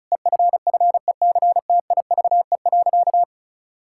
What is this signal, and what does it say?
Morse code, heard as a keyed tone, about 34 wpm, 700 Hz. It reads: EFFECTIVE.
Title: Sentences from Top 1000 Words - Review Part 6 - 35wpm